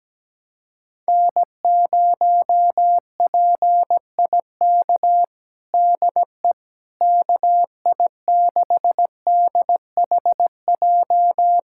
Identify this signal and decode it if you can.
Morse code, heard as a keyed tone, about 17 wpm, 700 Hz. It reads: N0PIK DE KI6DHJ
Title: USA Callsign DE Callsign 17wpm